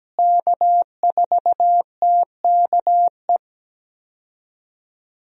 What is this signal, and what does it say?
Morse code, heard as a keyed tone, about 17 wpm, 700 Hz. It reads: K4TKE
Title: USA Call Signs - 5 Characters - Repeated 3x - 17wpm